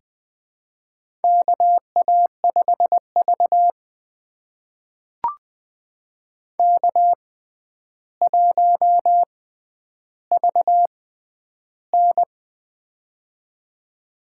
Morse code, heard as a keyed tone, about 20 wpm, spaced as for 10 wpm, 700 Hz. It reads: KA5V K1VN
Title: USA Call Signs - 4 Characters - 6x Character Spacing - 20wpm